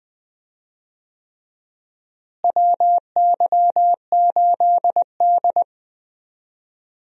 Morse code, heard as a keyed tone, about 20 wpm, 700 Hz. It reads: WY8D
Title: USA Call Signs - 4 Characters - Repeated 3x - 20wpm